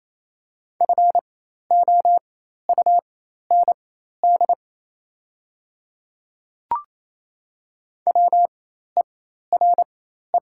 Morse code, heard as a keyed tone, about 27 wpm, spaced as for 13 wpm, 700 Hz. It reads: FOUND WERE
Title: 4x Character Spacing - Top 300 Words - 28wpm